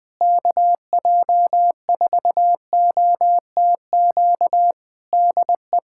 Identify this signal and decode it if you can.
Morse code, heard as a keyed tone, about 19 wpm, 700 Hz. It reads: KJ4OTQ DE
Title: Practice QSO - Exchange 4 - Ending - 20wpm